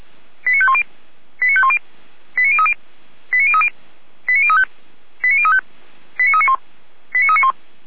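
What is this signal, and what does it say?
Five-tone selective call (ZVEI), heard as quick runs of five tones, tones of about 70 ms. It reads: 89510 89510 89030 89030 89035 89035 89391 89391
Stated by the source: Five-tone ZVEI selective call received by SDR.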